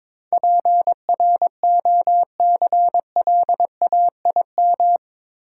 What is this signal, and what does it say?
Morse code, heard as a keyed tone, about 22 wpm, 700 Hz. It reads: PROCLAIM
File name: practice / 22wpm / Eight Letter Words - 22wpm